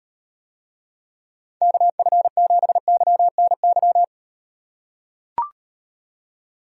Morse code, heard as a keyed tone, about 38 wpm, 700 Hz. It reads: KF7YNY